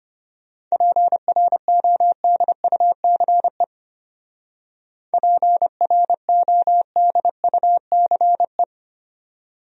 Morse code, heard as a keyed tone, about 32 wpm, 700 Hz. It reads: PRODUCE PRODUCE